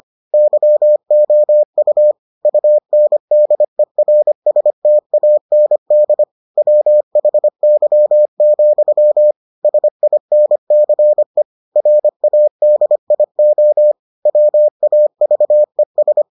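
Morse code, heard as a keyed tone, about 25 wpm, 600 Hz. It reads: YOU UNDERSTAND WHY, SINCE RADIO WAVES
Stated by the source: Morse edition of the Foundations of Amateur Radio episodes.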